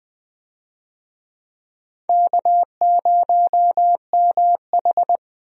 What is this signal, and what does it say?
Morse code, heard as a keyed tone, about 21 wpm, 700 Hz. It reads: K0MH
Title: USA Call Signs - 4 Characters - Repeated 3x - 20wpm